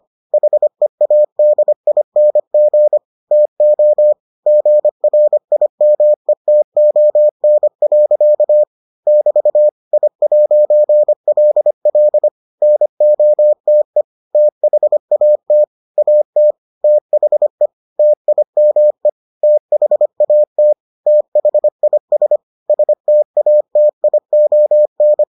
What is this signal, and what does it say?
Morse code, heard as a keyed tone, about 25 wpm, 600 Hz. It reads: HEADING TO GRIMETON. <BT> I'LL NOTE THAT AT THE TIME THAT THIS STATION